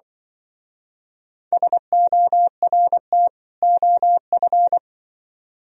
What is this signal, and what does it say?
Morse code, heard as a keyed tone, about 24 wpm, 700 Hz. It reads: SORT OF
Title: Speed-Racing - Sets of 2 Words 20wpm